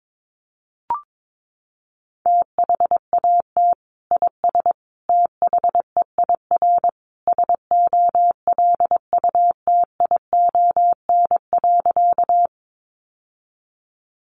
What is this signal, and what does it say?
Morse code, heard as a keyed tone, about 22 wpm, 700 Hz. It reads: THAT IS THEIR SOLUTION.
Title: Sentences from Top 1300 Words 22wpm